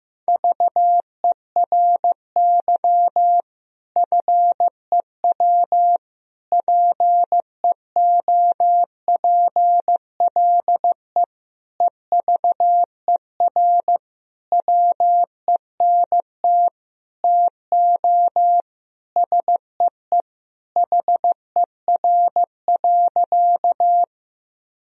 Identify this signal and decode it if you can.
Morse code, heard as a keyed tone, about 15 wpm, 700 Hz. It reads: VERY FEW PEOPLE EVER WENT TO SEE HER.
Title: Sentences from Top 1000 Words - Review Part 1 - 15wpm